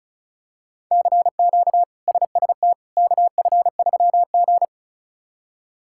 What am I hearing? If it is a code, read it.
Morse code, heard as a keyed tone, about 36 wpm, 700 Hz. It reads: CQ SST KF3G
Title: K1USN SST QSO Practice - 35wpm